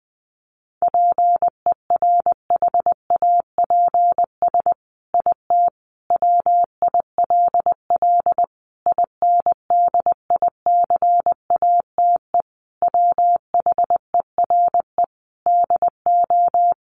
Morse code, heard as a keyed tone, about 20 wpm, 700 Hz. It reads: PERHAPS IT WILL INDICATE WHERE DO
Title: Sentences from Top 1300 Words 20wpm